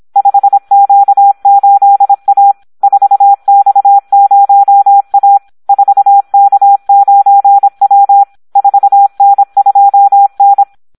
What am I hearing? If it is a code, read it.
Morse code, heard as a keyed tone, about 26 wpm, 800 Hz. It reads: 5Q8A 4X0A 4K9W 4N2N
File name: Audio practica CW / SD 1 26